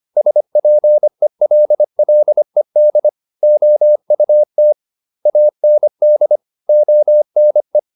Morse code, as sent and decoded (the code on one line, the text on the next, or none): ... .--. .-.. .-.. -.. / --- ..- - / .- -. -.. / --- -. .
SPELLED OUT AND ONE